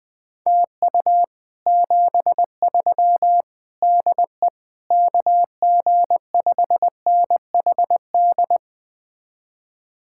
Morse code, heard as a keyed tone, about 20 wpm, 700 Hz. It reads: TU 73 DE KG5NHD